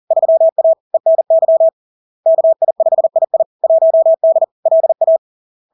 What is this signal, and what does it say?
Morse code, heard as a keyed tone, about 40 wpm, 650 Hz. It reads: 3A ENY KI5II 1D LA
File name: LARK-FD-2022-Log-in-CW-40wpm-650-Hz